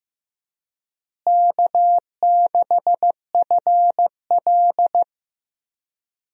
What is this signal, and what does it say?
Morse code, heard as a keyed tone, about 16 wpm, 700 Hz. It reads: K6FL
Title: USA Call Signs - 4 Characters - Repeated 3x - 15wpm